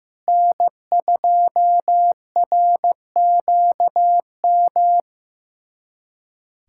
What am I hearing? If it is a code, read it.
Morse code, heard as a keyed tone, about 15 wpm, 700 Hz. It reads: N2RQM
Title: USA Callsign DE Callsign - 5 Characters - 15wpm